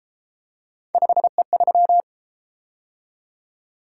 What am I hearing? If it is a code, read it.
Morse code, heard as a keyed tone, about 31 wpm, 700 Hz. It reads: 5E3